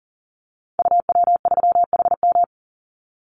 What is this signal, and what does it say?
Morse code, heard as a keyed tone, about 38 wpm, 700 Hz. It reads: UW3HM
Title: Call Signs - 40wpm